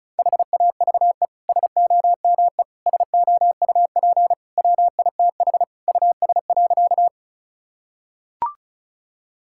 Morse code, heard as a keyed tone, about 36 wpm, 700 Hz. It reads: HAVE SOME SOUP WITH US.
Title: Sentences from Top 1000 Words - Review Part 6 - 35wpm